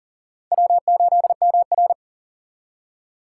Morse code, heard as a keyed tone, about 37 wpm, 700 Hz. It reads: W8MR
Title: USA Call Signs - 4 Characters - 6x Character Spacing - 40wpm